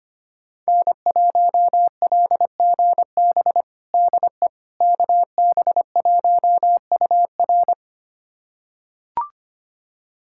Morse code, heard as a keyed tone, about 25 wpm, 700 Hz. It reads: N1LGB DE KB1UR
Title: USA Callsign DE Callsign - 5 Characters - 25wpm